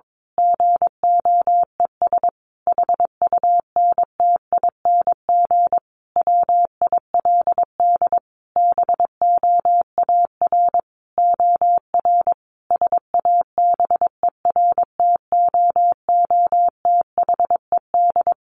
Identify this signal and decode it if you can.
Morse code, heard as a keyed tone, about 22 wpm, 700 Hz. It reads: GOES HUNTING WILD BOAR OR SABERTOOTHED
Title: A Sound Of Thunder 22wpm